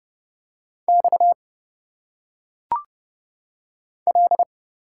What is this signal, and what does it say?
Morse code, heard as a keyed tone, about 30 wpm, 700 Hz. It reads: X L
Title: Single Letters - Phonetic Pronunciation 30wpm